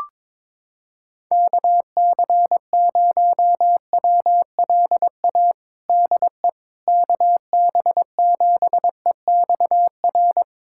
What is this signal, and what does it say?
Morse code, heard as a keyed tone, about 22 wpm, 700 Hz. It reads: KC0WLA DE KB7EXR
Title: USA Callsign DE Callsign 22wpm